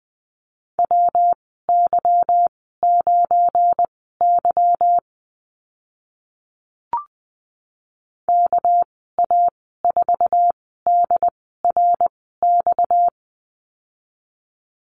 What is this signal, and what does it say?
Morse code, heard as a keyed tone, about 20 wpm, spaced as for 14 wpm, 700 Hz. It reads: WY9Y KA4DRX
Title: 238 Lesson 38g - Letter q - Call Signs 20wpm